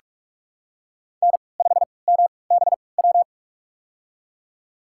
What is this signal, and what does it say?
Morse code, heard as a keyed tone, about 42 wpm, spaced as for 25 wpm, 700 Hz. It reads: N5MBW
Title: USA Call Signs - 5 Characters - 3x Character Spacing - 45wpm